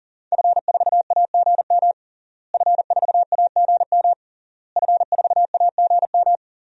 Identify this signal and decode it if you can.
Morse code, heard as a keyed tone, about 40 wpm, 700 Hz. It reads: F4AGM F4AGM F4AGM
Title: DXCC-100 Call Signs - 5 Characters - Repeated 3x - 40wpm